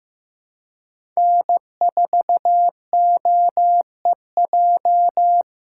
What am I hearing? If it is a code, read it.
Morse code, heard as a keyed tone, about 15 wpm, 700 Hz. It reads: N4OEJ
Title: USA Call Signs - 5 Characters - 4x Character Spacing - 15wpm